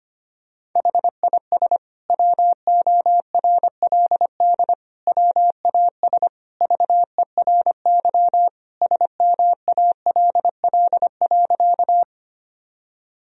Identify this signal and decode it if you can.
Morse code, heard as a keyed tone, about 25 wpm, 700 Hz. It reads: HIS WORLD WAS VERY SMALL.
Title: Sentences from Top 1000 Words - Review Part 1 - 25wpm